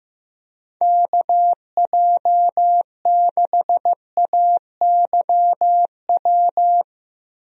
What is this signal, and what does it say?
Morse code, heard as a keyed tone, about 15 wpm, 700 Hz. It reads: KJ6AYW
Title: USA Call Signs - 6 Characters - Repeated 3x - 15wpm